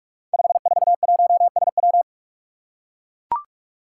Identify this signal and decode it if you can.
Morse code, heard as a keyed tone, about 44 wpm, 700 Hz. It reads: 541SW
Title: Five Challenging Letter-Number Combinations 45wpm